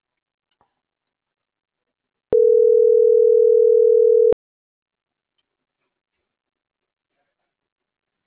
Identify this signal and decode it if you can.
Ringback tone (North America), 440 + 480 Hz.